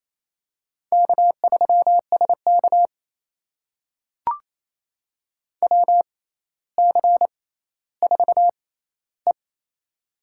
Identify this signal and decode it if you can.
Morse code, heard as a keyed tone, about 29 wpm, spaced as for 15 wpm, 700 Hz. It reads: K3SK WC4E
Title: USA Call Signs - 4 Characters - 6x Character Spacing - 28wpm